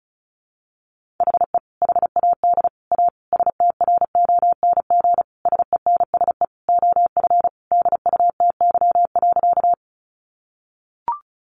Morse code, heard as a keyed tone, about 34 wpm, 700 Hz. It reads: HE HAD A STRONG SENSE OF DUTY.
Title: Sentences from Top 1500 Words 35wpm